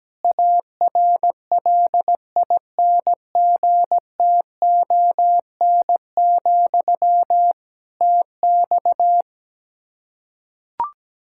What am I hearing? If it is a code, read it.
Morse code, heard as a keyed tone, about 17 wpm, 700 Hz. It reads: ARLINGTON, TX
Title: Practice QSO - Exchange 1 - Beginning - 17wpm